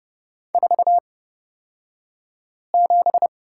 Morse code, heard as a keyed tone, about 29 wpm, spaced as for 5 wpm, 700 Hz. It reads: 47
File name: Lesson 38d - Letter q - Two Characters 30wpm